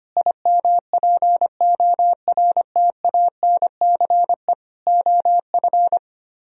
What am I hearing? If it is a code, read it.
Morse code, heard as a keyed tone, about 24 wpm, 700 Hz. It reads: IMPORTANCE OF
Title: Speed-Racing - Sets of 3 Words 25wpm